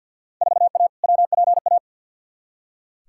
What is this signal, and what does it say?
Morse code, heard as a keyed tone, about 46 wpm, 700 Hz. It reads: 4A WPA
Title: Field Day Exchanges Encore - 50wpm